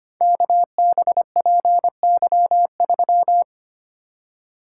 Morse code, heard as a keyed tone, about 25 wpm, 700 Hz. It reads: KBPY3
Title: Code Groups - 5 Letter:Numbers - 25wpm